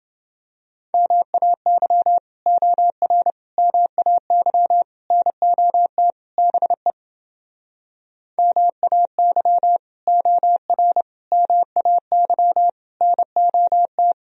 Morse code, heard as a keyed tone, about 29 wpm, 700 Hz. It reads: MAY OR MAY NOT BE MAY OR MAY NOT